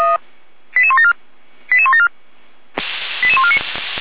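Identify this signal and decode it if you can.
Five-tone selective call (ZVEI), heard as quick runs of five tones, tones of about 70 ms. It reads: 88174 88174 88149
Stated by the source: ZVEI five-tone selective call received by SDR.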